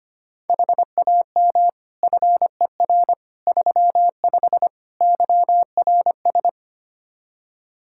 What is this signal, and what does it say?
Morse code, heard as a keyed tone, about 25 wpm, 700 Hz. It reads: HAM FER 35 YRS
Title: K1USN SST QSO Practice 25wpm